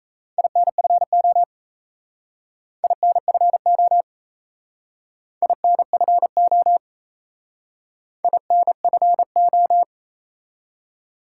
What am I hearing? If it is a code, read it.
Morse code, heard as a keyed tone, about 33 wpm, 700 Hz. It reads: INFO INFO INFO INFO
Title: Speed-Racing Top 180 QSO Elements 28wpm